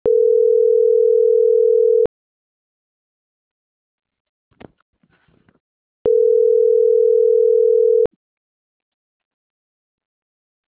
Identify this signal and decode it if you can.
Ringback tone (North America), 440 + 480 Hz.